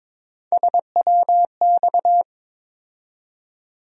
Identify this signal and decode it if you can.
Morse code, heard as a keyed tone, about 22 wpm, 700 Hz. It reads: SWX
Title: USA Call Sign Postfixes - 3 Letters 22wpm